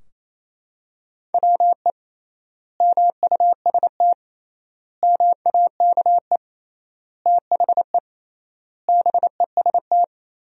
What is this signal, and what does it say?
Morse code, heard as a keyed tone, about 28 wpm, 700 Hz. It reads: WE MUST MAKE THE BEST
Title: 3x Word-Spacing - Sentences from Top 400 Words - 28wpm